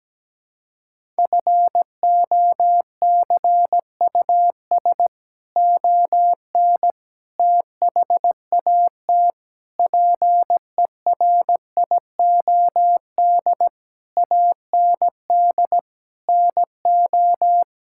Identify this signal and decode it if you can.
Morse code, heard as a keyed tone, about 17 wpm, 700 Hz. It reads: FOCUS ON THAT PERIOD AND NO
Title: Sentences from Top 1000 Words - Review Part 5 - 17wpm